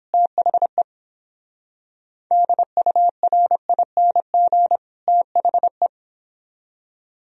Morse code, heard as a keyed tone, about 29 wpm, 700 Hz. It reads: THE DURING THE